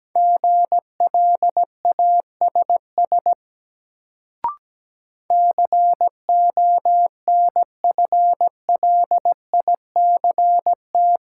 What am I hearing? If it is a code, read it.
Morse code, heard as a keyed tone, about 17 wpm, 700 Hz. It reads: GLASS CONFLICT
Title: Sentences from Top 1300 Words - New Word Drill Alt - 17wpm